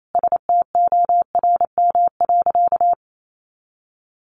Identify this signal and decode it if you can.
Morse code, heard as a keyed tone, about 28 wpm, 700 Hz. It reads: STORM.
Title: Sentences from Top 1500 Words 28wpm